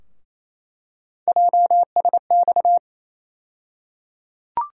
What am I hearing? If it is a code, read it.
Morse code, heard as a keyed tone, about 28 wpm, 700 Hz. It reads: JSX
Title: USA Call Sign Postfixes - 3 Letters 28wpm